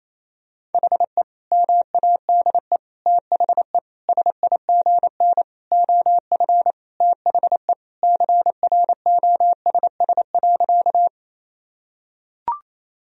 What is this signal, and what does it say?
Morse code, heard as a keyed tone, about 28 wpm, 700 Hz. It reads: HE MADE THE SIGN OF THE CROSS.
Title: Sentences from Top 1000 Words - Review Part 1 - 28wpm